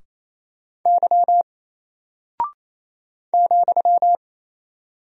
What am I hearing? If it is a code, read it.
Morse code, heard as a keyed tone, about 28 wpm, 700 Hz. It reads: Y ,
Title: Single Character (Letter:Number:Punctuation:Prosign) Alt 28wpm